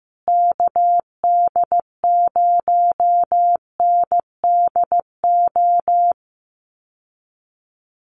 Morse code, heard as a keyed tone, about 15 wpm, 700 Hz. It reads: KD0NDO
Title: USA Call Signs Repeated 3x 15wpm